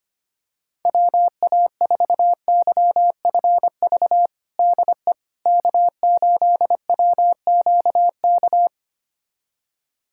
Morse code, heard as a keyed tone, about 25 wpm, 700 Hz. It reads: WA4YFV DE K8WQK
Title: USA Callsign DE Callsign 25wpm